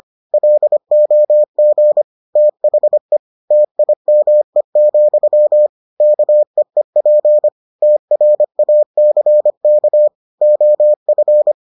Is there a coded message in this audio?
Yes: Morse code from 0.3 to 11.5 s, heard as a keyed tone, about 25 wpm, 600 Hz. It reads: LOG THE TIME, KEEP TRACK OF